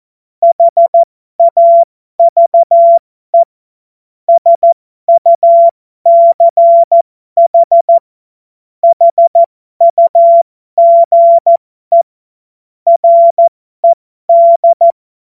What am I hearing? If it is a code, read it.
Morse code, heard as a keyed tone, about 14 wpm, spaced as for 12 wpm, 680 Hz. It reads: HAVE SUCH HUGE RED
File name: Morsekurs / Witze